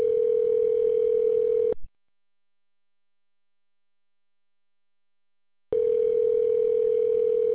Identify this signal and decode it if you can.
Ringback tone (North America), 440 + 480 Hz.